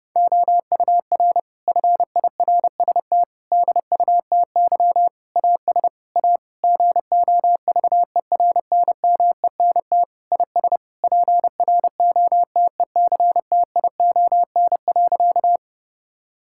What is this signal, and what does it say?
Morse code, heard as a keyed tone, about 30 wpm, 700 Hz. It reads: OUR FIRST DUTY AS A GOVERNMENT IS PROTECTION.
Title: Sentences from Top 1600 Words 30wpm